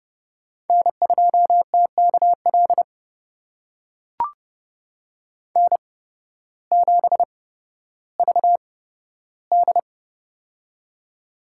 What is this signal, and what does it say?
Morse code, heard as a keyed tone, about 30 wpm, spaced as for 14 wpm, 700 Hz. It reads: N2TKL N7VD
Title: USA Call Signs - 8x Character Spacing - 30wpm